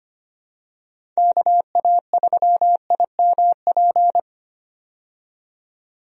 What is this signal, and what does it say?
Morse code, heard as a keyed tone, about 25 wpm, 700 Hz. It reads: KA3IMP
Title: USA Call Signs - 6 Characters - 25wpm